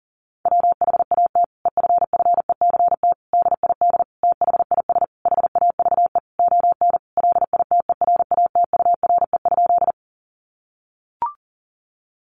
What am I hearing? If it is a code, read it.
Morse code, heard as a keyed tone, about 40 wpm, 700 Hz. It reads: WHAT EFFECT DID THIS HAVE ON LITERATURE?
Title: Sentences from Top 1000 Words 40wpm